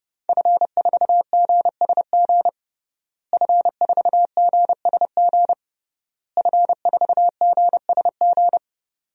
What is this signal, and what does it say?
Morse code, heard as a keyed tone, about 30 wpm, 700 Hz. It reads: F4GSG F4GSG F4GSG
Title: DXCC-100 Call Signs - 5 Characters - Repeated 3x - 30wpm